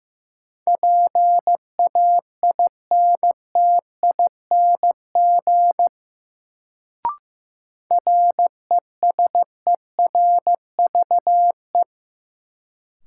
Morse code, heard as a keyed tone, about 16 wpm, 700 Hz. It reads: PAINTING RESERVE
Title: Sentences from Top 1000 Words - Review Part 6 - New Word Drill - 15wpm